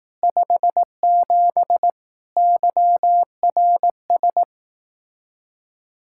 Morse code, heard as a keyed tone, about 18 wpm, 700 Hz. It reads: 57 YRS
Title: Speed-Racing Common QSO Phrases Encore - 15wpm